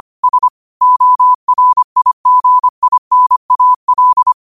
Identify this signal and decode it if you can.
Morse code, heard as a keyed tone, about 25 wpm, 1000 Hz. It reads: S ORIGINAL
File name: qotdfiles / 25wpm